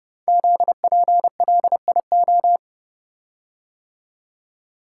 Morse code, heard as a keyed tone, about 30 wpm, 700 Hz. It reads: ZPLIO
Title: Code Groups - 5 Letters - 30wpm